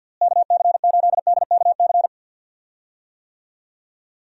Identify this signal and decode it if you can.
Morse code, heard as a keyed tone, about 52 wpm, 700 Hz. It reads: KC8DKC
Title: USA Call Signs - 6 Characters - Repeated 3x - 50wpm